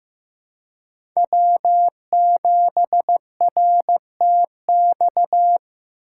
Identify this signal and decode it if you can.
Morse code, heard as a keyed tone, about 15 wpm, 700 Hz. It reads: W7RTX
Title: USA Call Signs - 5 Characters - 3x Character Spacing - 15wpm